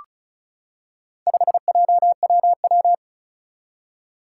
Morse code, heard as a keyed tone, about 36 wpm, 700 Hz. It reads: 5JWW
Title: Four Challenging Letter-Number Combinations 35wpm